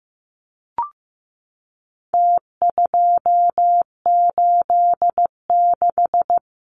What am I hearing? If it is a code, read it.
Morse code, heard as a keyed tone, about 15 wpm, 700 Hz. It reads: T286
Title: Numbers - Contest Style - 4-Digit 15wpm